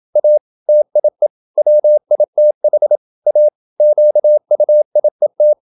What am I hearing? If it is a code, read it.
Morse code, heard as a keyed tone, about 27 wpm, 600 Hz. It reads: A TIE WITH A QUIET
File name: Never 0027wpm